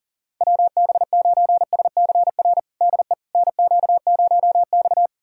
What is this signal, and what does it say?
Morse code, heard as a keyed tone, about 40 wpm, 700 Hz. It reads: WB9SCR DE NQ0X
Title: USA Callsign DE Callsign 40wpm